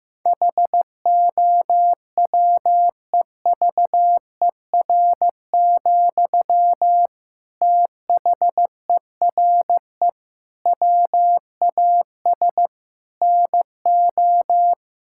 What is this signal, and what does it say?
Morse code, heard as a keyed tone, about 15 wpm, 700 Hz. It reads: HOWEVER, THERE WAS NO
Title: Sentences from Top 1600 Words 15wpm